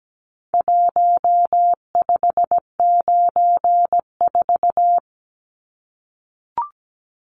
Morse code, heard as a keyed tone, about 17 wpm, 700 Hz. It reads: 1594
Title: Numbers - Contest Style - 4-Digit 17wpm